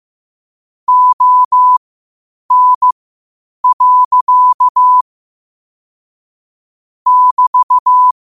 Morse code, heard as a keyed tone, about 15 wpm, spaced as for 8 wpm, 1000 Hz. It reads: ON. <BT>
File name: qotdfiles / 8wpm